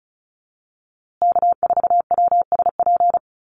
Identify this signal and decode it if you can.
Morse code, heard as a keyed tone, about 34 wpm, 700 Hz. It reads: K4WSP
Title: USA Call Signs Repeated 2x 35wpm